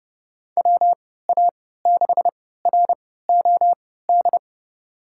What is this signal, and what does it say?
Morse code, heard as a keyed tone, about 32 wpm, spaced as for 17 wpm, 700 Hz. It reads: WA6ROD
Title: USA Call Signs - 6 Characters - 3x Character Spacing - 30wpm